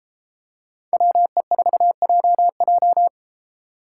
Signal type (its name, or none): Morse code